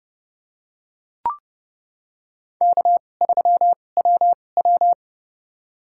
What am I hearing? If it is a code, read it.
Morse code, heard as a keyed tone, about 32 wpm, spaced as for 22 wpm, 700 Hz. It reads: K3WW
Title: USA Call Signs - 2x Character Spacing - 30wpm